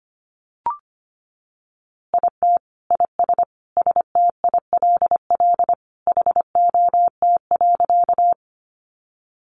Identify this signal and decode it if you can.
Morse code, heard as a keyed tone, about 25 wpm, 700 Hz. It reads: IT IS STILL HOT.
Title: Easy Phrases 25wpm